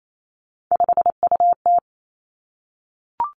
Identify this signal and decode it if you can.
Morse code, heard as a keyed tone, about 28 wpm, 700 Hz. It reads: 5UT
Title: Three Challenging Letter-Number Combinations 28wpm